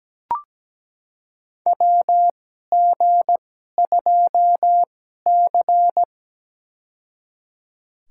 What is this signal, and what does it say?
Morse code, heard as a keyed tone, about 17 wpm, spaced as for 12 wpm, 700 Hz. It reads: WG2C
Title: USA Call Signs - 4 Characters - 2x Character Spacing - 17wpm